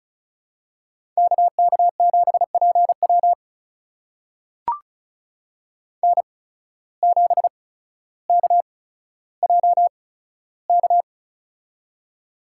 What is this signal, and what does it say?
Morse code, heard as a keyed tone, about 36 wpm, spaced as for 15 wpm, 700 Hz. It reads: KK7PW N7KJK
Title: USA Call Signs - 5 Characters - 8x Character Spacing - 35wpm